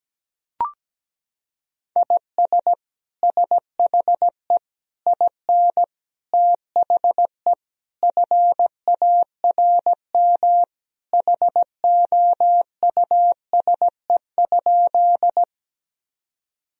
Morse code, heard as a keyed tone, about 17 wpm, 700 Hz. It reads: IS SHE IN THE FARM HOUSE?
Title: Sentences from Top 1000 Words - Review Part 2 - 17wpm